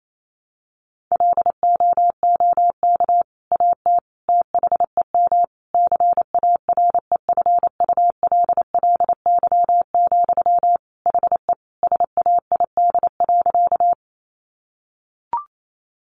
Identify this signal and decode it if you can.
Morse code, heard as a keyed tone, about 28 wpm, 700 Hz. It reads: LOOK AT THEM CAREFULLY, HE SAID.
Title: Sentences from Top 1300 Words 28wpm